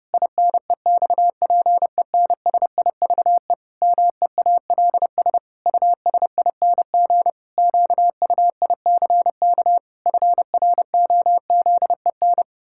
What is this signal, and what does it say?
Morse code, heard as a keyed tone, about 30 wpm, 700 Hz. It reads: INEXPENSIVE MEALS USING QUICK FROZEN
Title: Pangram Challenge v2 - 30wpm